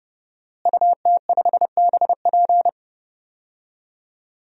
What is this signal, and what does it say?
Morse code, heard as a keyed tone, about 31 wpm, 700 Hz. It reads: UT5BP
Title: DXCC-100 Call Signs - 5 Characters - Repeated 3x - 30wpm